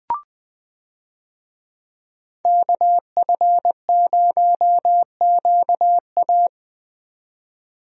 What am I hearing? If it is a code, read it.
Morse code, heard as a keyed tone, about 20 wpm, 700 Hz. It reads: KF0QA
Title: USA Call Signs - 5 Characters - 20wpm